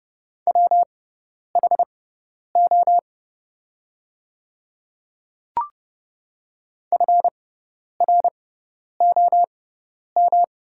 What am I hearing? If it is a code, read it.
Morse code, heard as a keyed tone, about 31 wpm, spaced as for 10 wpm, 700 Hz. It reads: WHO FROM
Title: 6x Character Spacing - Top 100 Words - 30wpm